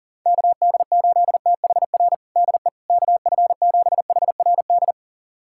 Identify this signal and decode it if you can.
Morse code, heard as a keyed tone, about 40 wpm, 700 Hz. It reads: KD8THR DE KF7HRD
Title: USA Callsign DE Callsign - 6 Characters - 40wpm